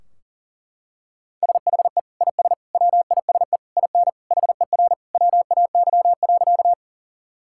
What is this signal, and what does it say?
Morse code, heard as a keyed tone, about 40 wpm, 700 Hz. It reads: SHE IS WISE IN HER WAY.
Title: Sentences from Top 1600 Words 40wpm